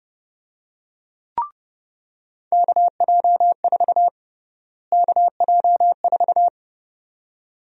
Morse code, heard as a keyed tone, about 30 wpm, 700 Hz. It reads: KJ4 KJ4
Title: USA Call Sign Prefixes - Repeated 2x - 30wpm